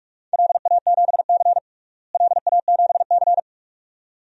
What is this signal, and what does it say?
Morse code, heard as a keyed tone, about 46 wpm, 700 Hz. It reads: LA7C LA7C
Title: DXCC-100 Call Signs - 4 Characters - Repeated 2x - 45wpm